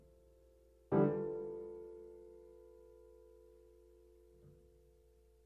Dial tone (North America), 350 + 440 Hz.